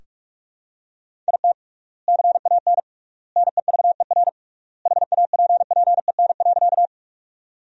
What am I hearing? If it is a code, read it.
Morse code, heard as a keyed tone, about 46 wpm, 700 Hz. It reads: IT CAN NEVER HAPPEN.
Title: 3x Word-Spacing - Sentences from Top 400 Words - 45wpm